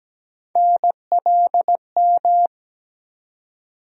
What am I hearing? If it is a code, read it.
Morse code, heard as a keyed tone, about 17 wpm, 700 Hz. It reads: NLM
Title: USA Call Sign Postfixes - 3 Letters 17wpm